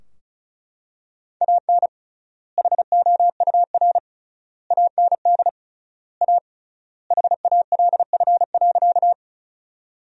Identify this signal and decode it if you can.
Morse code, heard as a keyed tone, about 34 wpm, 700 Hz. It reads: AN HOUR AND A HALF.
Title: 3x Word-Spacing - Sentences from Top 400 Words - 35wpm